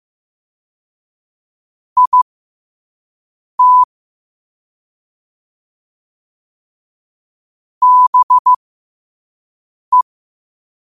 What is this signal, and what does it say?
Morse code, heard as a keyed tone, about 15 wpm, spaced as for 5 wpm, 1000 Hz. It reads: IT BE
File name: qotdfiles / 5wpm